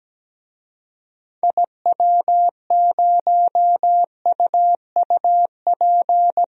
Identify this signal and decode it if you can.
Morse code, heard as a keyed tone, about 17 wpm, 700 Hz. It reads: IW0UUP
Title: DXCC-100 Call Signs - 6 Characters - 17wpm